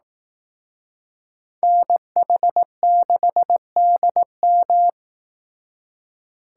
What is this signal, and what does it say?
Morse code, heard as a keyed tone, about 18 wpm, 700 Hz. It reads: NH6DM
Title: Speed Racing - USA 2x2 Call Signs 15wpm